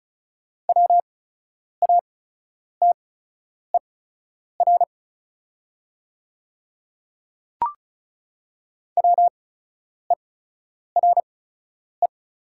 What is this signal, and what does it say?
Morse code, heard as a keyed tone, about 33 wpm, spaced as for 9 wpm, 690 Hz. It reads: WATER WERE
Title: 8x Character Spacing - Top 300 Words - 35wpm